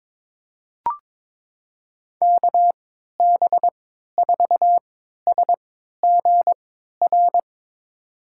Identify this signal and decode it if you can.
Morse code, heard as a keyed tone, about 22 wpm, spaced as for 13 wpm, 700 Hz. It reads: KB4SGR
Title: USA Call Signs - 6 Characters - 3x Character Spacing - 22wpm